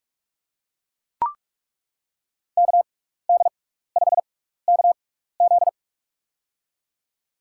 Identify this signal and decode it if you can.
Morse code, heard as a keyed tone, about 42 wpm, spaced as for 15 wpm, 700 Hz. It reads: KD5KZ
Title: USA Call Signs - 5 Characters - 6x Character Spacing - 45wpm